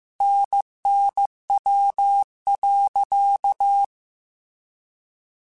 Morse code, heard as a keyed tone, about 15 wpm, 800 Hz. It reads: NNW.
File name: Weather Sets / Weather Sets-mp3s / Weather 5 22 @13